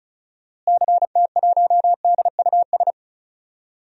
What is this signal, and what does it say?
Morse code, heard as a keyed tone, about 36 wpm, 700 Hz. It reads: CT1DUS
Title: DXCC-100 Call Signs - 6 Characters - Repeated 2x - 35wpm